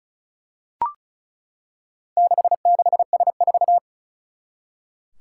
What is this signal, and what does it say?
Morse code, heard as a keyed tone, about 36 wpm, 700 Hz. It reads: 66S4
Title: Four Challenging Letter-Number Combinations 35wpm